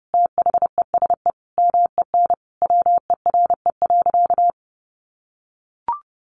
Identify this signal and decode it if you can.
Morse code, heard as a keyed tone, about 30 wpm, 700 Hz. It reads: THESE MEN WERE.
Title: Sentences from Top 1000 Words 30wpm